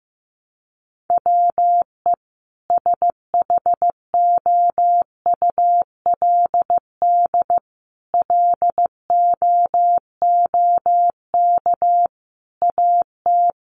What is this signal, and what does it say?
Morse code, heard as a keyed tone, about 15 wpm, 700 Hz. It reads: WE SHOULD LOOK AT
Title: Sentences from Top 300 Words 15wpm